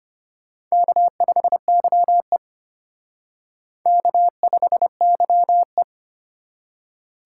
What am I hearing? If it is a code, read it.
Morse code, heard as a keyed tone, about 33 wpm, 700 Hz. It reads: K5YE K5YE